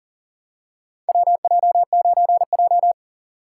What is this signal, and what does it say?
Morse code, heard as a keyed tone, about 40 wpm, 700 Hz. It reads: WJ9J
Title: USA Call Signs - 4 Characters - 6x Character Spacing - 40wpm